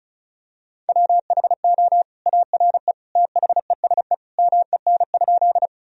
Morse code, heard as a keyed tone, about 34 wpm, 700 Hz. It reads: WHO ARE THESE MEN?
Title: Sentences from Top 1000 Words - Review Part 1 - 35wpm